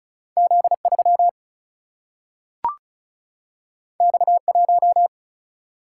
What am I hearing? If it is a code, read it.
Morse code, heard as a keyed tone, about 35 wpm, 700 Hz. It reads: Z3 X1
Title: Single-Letter Single-Number 35wpm